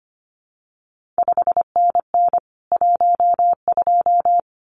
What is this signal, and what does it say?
Morse code, heard as a keyed tone, about 25 wpm, 700 Hz. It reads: 5NN 12
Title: CQ WW DX Contest Exchange Encore - 25wpm